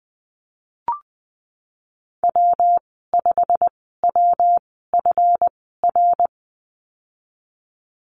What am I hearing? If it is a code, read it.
Morse code, heard as a keyed tone, about 20 wpm, spaced as for 15 wpm, 700 Hz. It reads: W5WFR
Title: 140 Lesson 23g - Letter f - Call Signs 20wpm